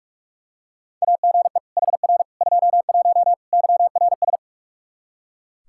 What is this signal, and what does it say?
Morse code, heard as a keyed tone, about 46 wpm, 700 Hz. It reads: AGE HR 21 YRS
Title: K1USN SST QSO Practice 45wpm